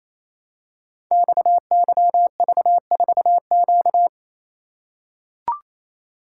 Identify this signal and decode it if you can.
Morse code, heard as a keyed tone, about 27 wpm, 700 Hz. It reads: XYV4Q
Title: Code Groups - 5 Letter:Numbers - 28wpm